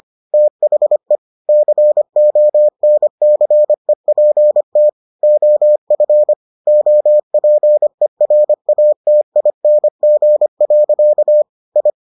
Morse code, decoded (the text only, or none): THE CONCEPT OF OPERATING. I